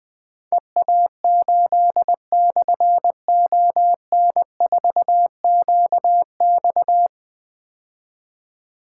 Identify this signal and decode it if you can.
Morse code, heard as a keyed tone, about 20 wpm, 700 Hz. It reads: EA8/ON4QX
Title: Difficult Call Signs 20wpm